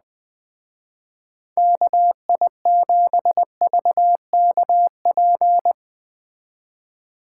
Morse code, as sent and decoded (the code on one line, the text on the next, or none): -.- .. --... ...- -.- .--.
KI7VKP